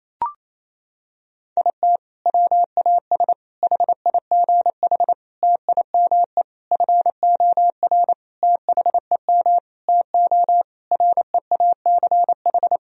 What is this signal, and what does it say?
Morse code, heard as a keyed tone, about 28 wpm, 700 Hz. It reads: IT WAS HIGH TIME FOR THEM TO REACH
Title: Sentences from Top 1000 Words - Review Part 2 - 28wpm